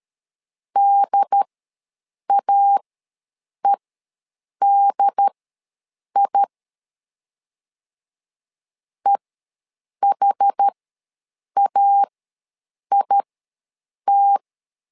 Morse code, heard as a keyed tone, about 13 wpm, spaced as for 7 wpm, 800 Hz. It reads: DAEDI EHAIT
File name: LEZIONI / 2 UHDN / 2 UHD ws 12wpm 1722